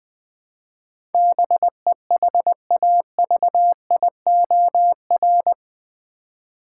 Morse code, heard as a keyed tone, about 20 wpm, 700 Hz. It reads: BEHAVIOR
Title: Build A Word - 4 Letters At A Time - 10-Letter Words - 20wpm